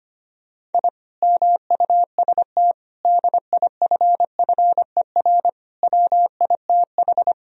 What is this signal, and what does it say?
Morse code, heard as a keyed tone, about 25 wpm, 700 Hz. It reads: I MUST DIFFER WITH
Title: Sentences from Top 1000 Words - Review Part 1 - 25wpm